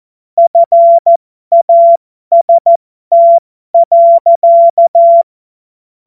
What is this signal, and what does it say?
Morse code, heard as a keyed tone, about 14 wpm, spaced as for 12 wpm, 680 Hz. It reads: FAST.
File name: Morsekurs / Witze